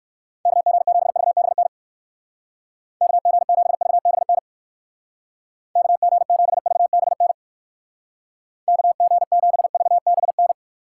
Morse code, heard as a keyed tone, about 59 wpm, 700 Hz. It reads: KG7VBN KG7VBN KG7VBN KG7VBN